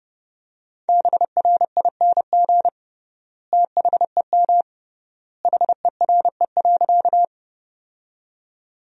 Morse code, heard as a keyed tone, about 30 wpm, 700 Hz. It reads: BRING THEM HERE.
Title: 3x Word-Spacing - Sentences from Top 500 Words - 30wpm